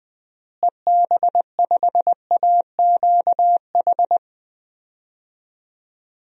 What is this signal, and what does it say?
Morse code, heard as a keyed tone, about 21 wpm, 700 Hz. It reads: EB5AQH
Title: DXCC-100 Call Signs - 6 Characters - Repeated 3x - 20wpm